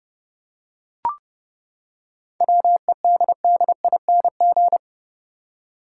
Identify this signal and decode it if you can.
Morse code, heard as a keyed tone, about 32 wpm, 700 Hz. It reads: WEDDING
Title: Seven Letter Words - 30wpm